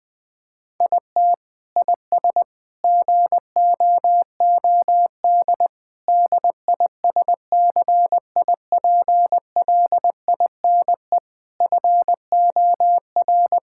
Morse code, heard as a keyed tone, about 20 wpm, 700 Hz. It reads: IT IS GOOD DISCIPLINE FOR
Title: Sentences from Top 1000 Words - Review Part 6 - 20wpm